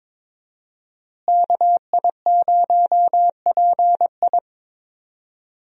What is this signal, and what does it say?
Morse code, heard as a keyed tone, about 22 wpm, 700 Hz. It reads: KI0PI
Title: USA Call Signs - 5 Characters - Repeated 2x - 22wpm